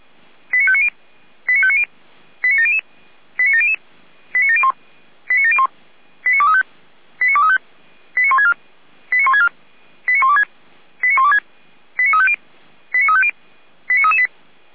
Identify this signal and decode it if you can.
Five-tone selective call (ZVEI), heard as quick runs of five tones, tones of about 70 ms. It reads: 89590 89590 89700 89700 89701 89701 89235 89235 89164 89164 89126 89126 89350 89350 89338